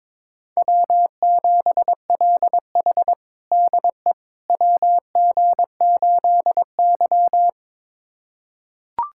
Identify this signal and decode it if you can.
Morse code, heard as a keyed tone, about 22 wpm, 700 Hz. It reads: W7LH DE WG8Y
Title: USA Callsign DE Callsign - 4 Characters - 22wpm